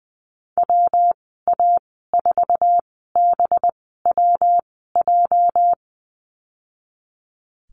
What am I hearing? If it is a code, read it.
Morse code, heard as a keyed tone, about 20 wpm, spaced as for 15 wpm, 700 Hz. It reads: WA4BWJ
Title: 231 Lesson 37g - Letter x - Call Signs 20wpm